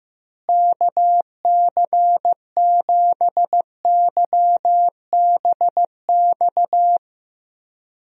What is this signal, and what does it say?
Morse code, heard as a keyed tone, about 15 wpm, 700 Hz. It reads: KC7YBX
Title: USA Call Signs - 6 Characters - Repeated 3x - 15wpm